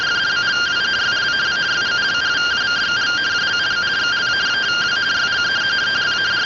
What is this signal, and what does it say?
Two-tone teleprinter signal, about 170 Hz shift.